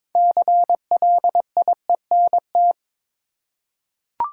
Morse code, heard as a keyed tone, about 22 wpm, 700 Hz. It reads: CLIENT
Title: Six Letter Words - 22wpm